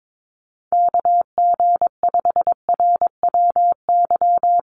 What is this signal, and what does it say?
Morse code, heard as a keyed tone, about 22 wpm, 700 Hz. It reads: KG5RWY
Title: USA Call Signs 22wpm